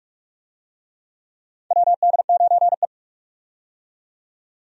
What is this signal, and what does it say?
Morse code, heard as a keyed tone, about 46 wpm, 700 Hz. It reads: WD9E
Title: USA Call Signs - 4 Characters - 45wpm